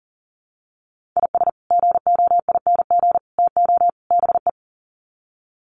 Morse code, heard as a keyed tone, about 37 wpm, 700 Hz. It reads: IS GOING TO BE
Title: Sets of 4 Words 40wpm